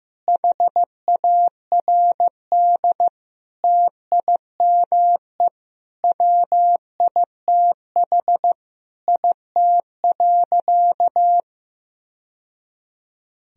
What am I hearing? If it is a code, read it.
Morse code, heard as a keyed tone, about 15 wpm, 700 Hz. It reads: HARD TIME WITH IT.
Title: Sentences from Top 1000 Words - Review Part 1 - 15wpm